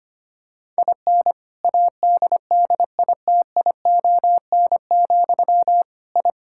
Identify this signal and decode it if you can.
Morse code, heard as a keyed tone, about 25 wpm, 700 Hz. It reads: IN ADDITION, I
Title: Sentences from Top 1000 Words - Review Part 6 - 25wpm